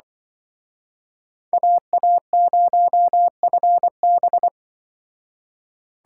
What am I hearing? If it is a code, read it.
Morse code, heard as a keyed tone, about 24 wpm, 700 Hz. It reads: AA0FB
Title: Speed Racing - USA 2x2 Call Signs 20wpm